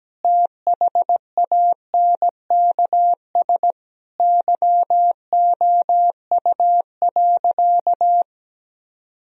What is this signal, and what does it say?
Morse code, heard as a keyed tone, about 17 wpm, 700 Hz. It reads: THANKS YOU.
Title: Sentences from Top 1000 Words - Review Part 3 - 17wpm